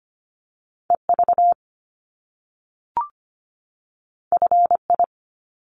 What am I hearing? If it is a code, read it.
Morse code, heard as a keyed tone, about 24 wpm, 700 Hz. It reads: EV FI
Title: Top Two Letter Combinations in Words 25wpm